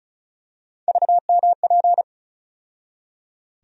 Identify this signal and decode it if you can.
Morse code, heard as a keyed tone, about 36 wpm, 700 Hz. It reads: VMP
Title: USA Call Sign Postfixes - 3 Letters 35wpm